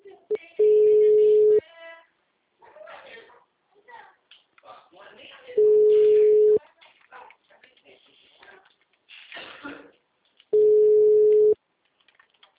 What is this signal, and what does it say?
Ringback tone (Europe, 425 Hz), with speech.